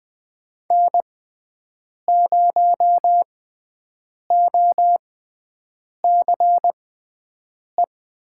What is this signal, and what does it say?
Morse code, heard as a keyed tone, about 20 wpm, spaced as for 7 wpm, 700 Hz. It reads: N0OCE
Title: USA Call Signs - 6x Character Spacing - 20wpm